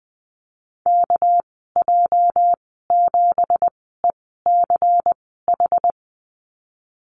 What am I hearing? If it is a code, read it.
Morse code, heard as a keyed tone, about 20 wpm, spaced as for 15 wpm, 700 Hz. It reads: KJ7ECH
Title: 213 Lesson 34g - Number 8 - Call Signs 20wpm